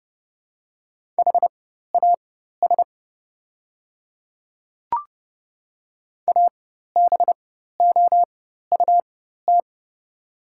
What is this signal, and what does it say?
Morse code, heard as a keyed tone, about 31 wpm, spaced as for 14 wpm, 700 Hz. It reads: HAS ABOUT
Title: 4x Character Spacing - Top 200 Words - 30wpm